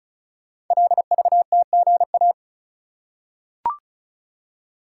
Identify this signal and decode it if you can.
Morse code, heard as a keyed tone, about 36 wpm, 700 Hz. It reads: LVTGA
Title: Code Groups - 5 Letters - 35wpm